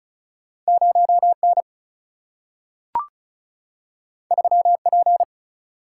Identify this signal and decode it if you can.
Morse code, heard as a keyed tone, about 36 wpm, 700 Hz. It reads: 0N 3P
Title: Single-Letter Single-Number 35wpm